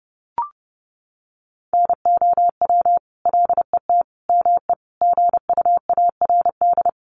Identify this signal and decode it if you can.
Morse code, heard as a keyed tone, about 30 wpm, 700 Hz. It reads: NOW LET ME GUARD
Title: Sentences from Top 700 Words 30wpm